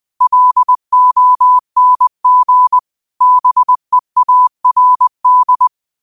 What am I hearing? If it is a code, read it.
Morse code, heard as a keyed tone, about 20 wpm, 1000 Hz. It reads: LONG BEARD
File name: qotdfiles / 20wpm